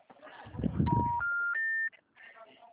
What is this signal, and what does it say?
Special information tone.